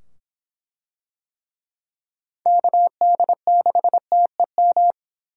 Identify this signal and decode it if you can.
Morse code, heard as a keyed tone, about 24 wpm, 700 Hz. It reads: KD6TEM